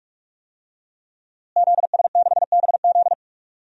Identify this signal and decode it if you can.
Morse code, heard as a keyed tone, about 46 wpm, 700 Hz. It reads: ZS6BZ
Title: DXCC-100 Call Signs - 5 Characters - Repeated 2x - 45wpm